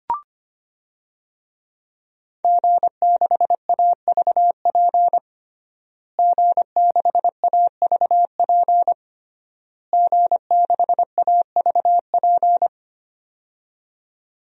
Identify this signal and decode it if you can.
Morse code, heard as a keyed tone, about 25 wpm, 700 Hz. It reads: G6AVP G6AVP G6AVP
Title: DXCC-100 Call Signs - 5 Characters - Repeated 3x - 25wpm